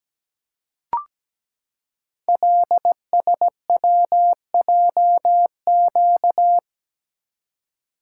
Morse code, heard as a keyed tone, about 17 wpm, 700 Hz. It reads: LSWJQ
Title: Code Groups - 5 Letter:Numbers - 17wpm